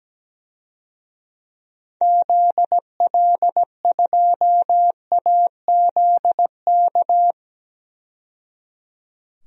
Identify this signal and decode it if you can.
Morse code, heard as a keyed tone, about 17 wpm, 700 Hz. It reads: ZL2AZK
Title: DXCC-100 Call Signs - 6 Characters - 17wpm